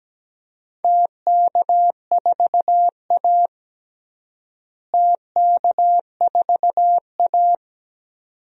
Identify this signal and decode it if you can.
Morse code, heard as a keyed tone, about 17 wpm, 700 Hz. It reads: TK4A TK4A
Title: DXCC-100 Call Signs - 4 Characters - Repeated 2x - 17wpm